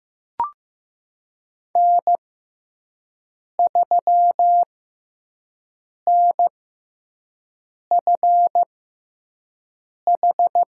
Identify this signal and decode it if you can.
Morse code, heard as a keyed tone, about 15 wpm, spaced as for 5 wpm, 700 Hz. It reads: N3NFH
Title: USA Call Signs - 5 Characters - 6x Character Spacing - 15wpm